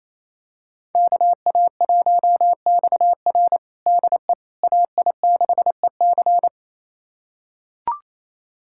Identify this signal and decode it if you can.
Morse code, heard as a keyed tone, about 28 wpm, 700 Hz. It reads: KA1XR DE AI6EC
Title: USA Callsign DE Callsign - 5 Characters - 28wpm